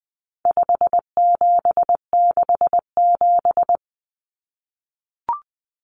Morse code, heard as a keyed tone, about 20 wpm, 700 Hz. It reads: 5767
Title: Common Cut Numbers - 4-Digit 20wpm